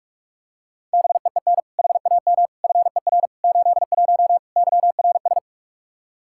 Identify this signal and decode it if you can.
Morse code, heard as a keyed tone, about 42 wpm, 700 Hz. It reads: BEEN HAM FER 81 YRS
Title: Practice QSO - Exchange 3 - More Info - 45wpm